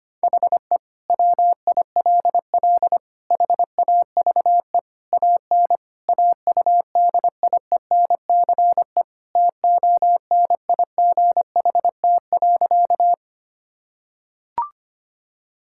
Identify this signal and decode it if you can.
Morse code, heard as a keyed tone, about 25 wpm, 700 Hz. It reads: HE WILL HAVE AN AUDIENCE TONIGHT.
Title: Sentences from Top 1000 Words - Review Part 5 - 25wpm